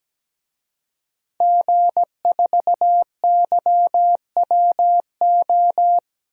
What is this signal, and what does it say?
Morse code, heard as a keyed tone, about 17 wpm, 700 Hz. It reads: G4YWO
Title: DXCC-100 Call Signs - 5 Characters - Repeated 3x - 17wpm